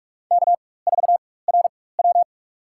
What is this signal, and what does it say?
Morse code, heard as a keyed tone, about 46 wpm, spaced as for 21 wpm, 700 Hz. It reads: K4RW
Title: USA Call Signs - 4x Character Spacing - 45wpm